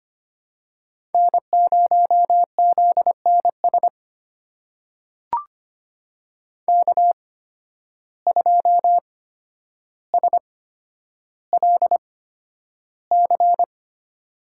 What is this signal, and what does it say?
Morse code, heard as a keyed tone, about 25 wpm, spaced as for 11 wpm, 700 Hz. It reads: N0ZNS K2SLC
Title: USA Call Signs - 5 Characters - 8x Character Spacing - 25wpm